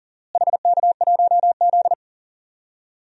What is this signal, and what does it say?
Morse code, heard as a keyed tone, about 40 wpm, 700 Hz. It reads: HK1Z
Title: DXCC-100 Call Signs - 4 Characters - Repeated 3x - 40wpm